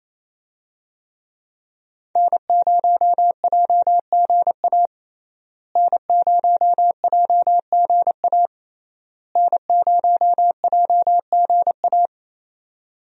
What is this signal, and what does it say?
Morse code, heard as a keyed tone, about 28 wpm, 700 Hz. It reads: N0JGA N0JGA N0JGA
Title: USA Call Signs - 5 Characters - Repeated 3x - 28wpm